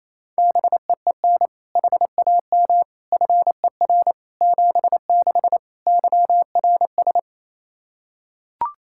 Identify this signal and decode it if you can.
Morse code, heard as a keyed tone, about 28 wpm, 700 Hz. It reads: BEEN HAM FER 76 YRS
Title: Practice QSO - Exchange 3 - More Info - 28wpm